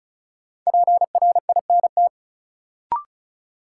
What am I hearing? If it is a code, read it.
Morse code, heard as a keyed tone, about 36 wpm, 700 Hz. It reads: PRINT
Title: Sentences from Top 1400 Words - New Word Drill Alt - 35wpm